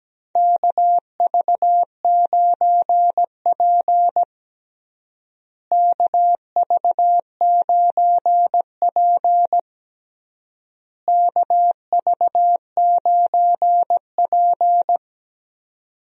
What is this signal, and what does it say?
Morse code, heard as a keyed tone, about 17 wpm, 700 Hz. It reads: KV9P KV9P KV9P
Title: USA Call Signs - 4 Characters - Repeated 3x - 17wpm